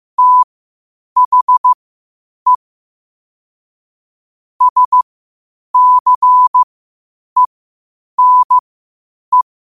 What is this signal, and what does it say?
Morse code, heard as a keyed tone, about 15 wpm, spaced as for 8 wpm, 1000 Hz. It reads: THE SCENE
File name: qotdfiles / 8wpm